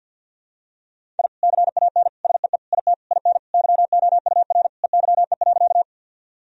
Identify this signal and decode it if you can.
Morse code, heard as a keyed tone, about 49 wpm, 700 Hz. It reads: I CAN SEE IT IN YOUR EYE.
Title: Sentences from Top 1000 Words - Review Part 1 - 50wpm